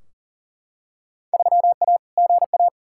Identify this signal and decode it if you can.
Morse code, heard as a keyed tone, about 40 wpm, 700 Hz. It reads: 3A GA
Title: Field Day Exchanges - 40wpm